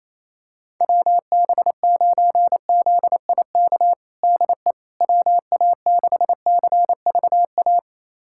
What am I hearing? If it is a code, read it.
Morse code, heard as a keyed tone, about 28 wpm, 700 Hz. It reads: WB9ZIK DE WA6CVA
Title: USA Callsign DE Callsign - 6 Characters - 28wpm